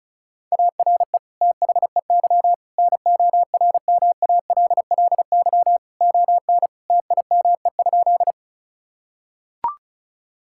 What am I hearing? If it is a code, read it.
Morse code, heard as a keyed tone, about 36 wpm, 700 Hz. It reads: ARE THEY NORMALLY ON TIME?
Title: Sentences from Top 1000 Words - Review Part 6 - 35wpm